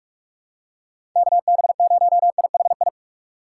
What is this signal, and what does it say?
Morse code, heard as a keyed tone, about 41 wpm, 700 Hz. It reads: KB0IHI
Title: USA Call Signs - 2x Character Spacing - 45wpm